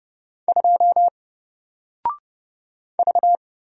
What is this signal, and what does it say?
Morse code, heard as a keyed tone, about 30 wpm, 700 Hz. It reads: 2 V
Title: Single Letter-Number Alt 30wpm